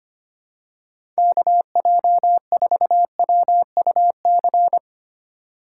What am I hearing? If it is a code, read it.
Morse code, heard as a keyed tone, about 25 wpm, 700 Hz. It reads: KJ4WUC
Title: USA Call Signs - 6x Character Spacing - 25wpm